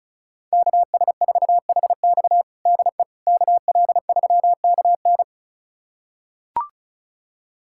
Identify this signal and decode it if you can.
Morse code, heard as a keyed tone, about 36 wpm, 700 Hz. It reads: KS4HX DE KL3KN
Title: USA Callsign DE Callsign - 5 Characters - 35wpm